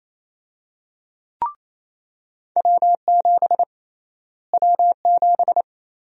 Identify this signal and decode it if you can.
Morse code, heard as a keyed tone, about 28 wpm, 700 Hz. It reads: W7 W7
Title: USA Call Sign Prefixes - Repeated 2x - 28wpm